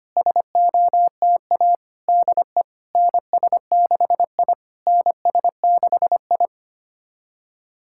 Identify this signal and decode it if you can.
Morse code, heard as a keyed tone, about 25 wpm, 700 Hz. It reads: SOTA DE NS6I NS6I
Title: SOTA QSO Practice 25wpm